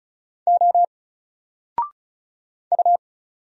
Morse code, heard as a keyed tone, about 36 wpm, 700 Hz. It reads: O U
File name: Single-Letter-Number-35wpm